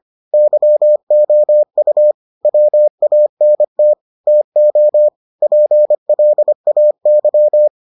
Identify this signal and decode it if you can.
Morse code, heard as a keyed tone, about 25 wpm, 600 Hz. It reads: YOU WANT TO PLAY